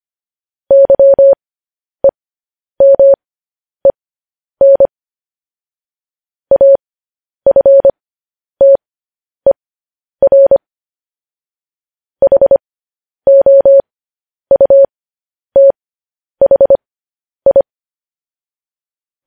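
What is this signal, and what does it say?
Morse code, heard as a keyed tone, about 25 wpm, spaced as for 10 wpm, 550 Hz. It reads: YEMEN AFTER HOUTHI
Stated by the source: Morse code practice of news headlines at 10 wpm.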